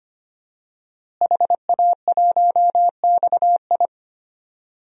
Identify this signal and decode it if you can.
Morse code, heard as a keyed tone, about 25 wpm, 700 Hz. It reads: HA1XI
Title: DXCC-100 Call Signs - 5 Characters - Repeated 3x - 25wpm